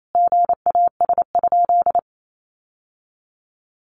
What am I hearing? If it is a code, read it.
Morse code, heard as a keyed tone, about 28 wpm, 700 Hz. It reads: GAS?
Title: Sentences from Top 1100 Words 28wpm